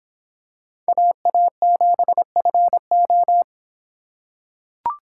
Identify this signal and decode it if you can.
Morse code, heard as a keyed tone, about 26 wpm, 700 Hz. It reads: AA7FO